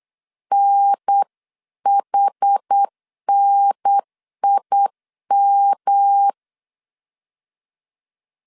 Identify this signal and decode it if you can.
Morse code, heard as a keyed tone, about 8 wpm, 800 Hz. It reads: NHNIM